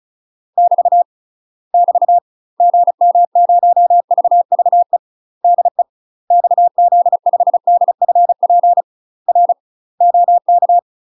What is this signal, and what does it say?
Morse code, heard as a keyed tone, about 34 wpm, 700 Hz. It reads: <BT> <BT> GM0VVE DE XZ5DFP R OK
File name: UK MASTER QSOs TRAINING / 35WPM / UK MASTER QSOs 0015